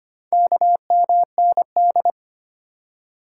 Morse code, heard as a keyed tone, about 25 wpm, 700 Hz. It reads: KMND
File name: practice / 25wpm / Four Letter-Number Combinations - 25wpm